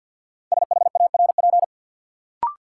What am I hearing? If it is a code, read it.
Morse code, heard as a keyed tone, about 52 wpm, 700 Hz. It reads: SHARP
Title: Sentences from Top 1600 Words - New Word Drill - 50wpm